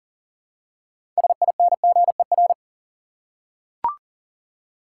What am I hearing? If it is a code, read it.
Morse code, heard as a keyed tone, about 40 wpm, 700 Hz. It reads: SINGER
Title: Six Letter Words - 40wpm